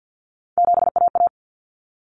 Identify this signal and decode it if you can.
Morse code, heard as a keyed tone, about 50 wpm, 700 Hz. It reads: 7AA
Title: Common Cut Numbers - 3-Digit 50wpm